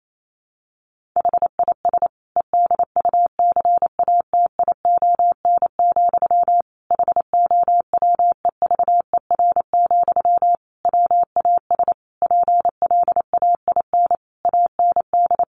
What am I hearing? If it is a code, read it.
Morse code, heard as a keyed tone, about 28 wpm, 700 Hz. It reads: HIS EDUCATION, HOWEVER, WAS PLAIN AND
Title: Sentences from Top 1400 Words 28wpm